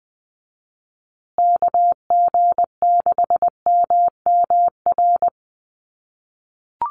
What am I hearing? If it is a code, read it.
Morse code, heard as a keyed tone, about 20 wpm, 700 Hz. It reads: KG6MMR
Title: USA Call Signs Repeated 3x 20wpm